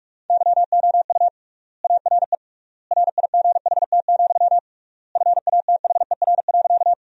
Morse code, heard as a keyed tone, about 42 wpm, 700 Hz. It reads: YOU ARE RIGHT, FATHER.
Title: 3x Word-Spacing - Sentences from Top 300 Words - 45wpm